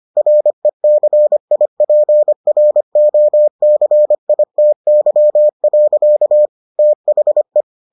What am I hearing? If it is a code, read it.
Morse code, heard as a keyed tone, about 25 wpm, 600 Hz. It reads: RECIPROCITY. THE